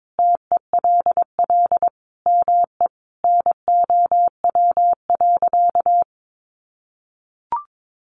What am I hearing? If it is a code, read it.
Morse code, heard as a keyed tone, about 22 wpm, 700 Hz. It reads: TELL ME NOW.
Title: Sentences from Top 1100 Words 22wpm